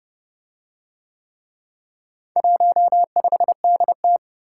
Morse code, heard as a keyed tone, about 26 wpm, 700 Hz. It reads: 15DT